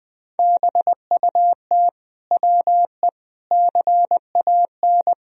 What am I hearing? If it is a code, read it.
Morse code, heard as a keyed tone, about 20 wpm, 700 Hz. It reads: BUT WE CAN
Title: Sentences from Top 1000 Words - Review Part 4 - 20wpm